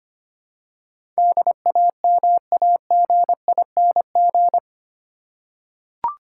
Morse code, heard as a keyed tone, about 25 wpm, 700 Hz. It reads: DAMAGING
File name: 8 Letter Words - First 3 Letters Given - 25wpm